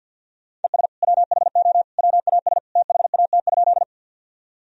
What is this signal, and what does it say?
Morse code, heard as a keyed tone, about 51 wpm, 700 Hz. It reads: ES WHO WAS THAT?
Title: Sentences from Top 100 Words and Abbreviations 50wpm